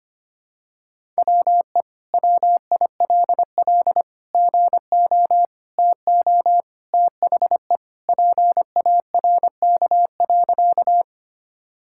Morse code, heard as a keyed tone, about 25 wpm, 700 Hz. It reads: WE WILL GO TO THE PARK.
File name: practice / 25wpm / Sentences with 2-Word Phrases - 25wpm